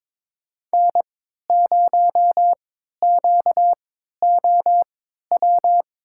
Morse code, heard as a keyed tone, about 21 wpm, spaced as for 12 wpm, 700 Hz. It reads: N0QOW
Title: USA Call Signs - 3x Character Spacing - 22wpm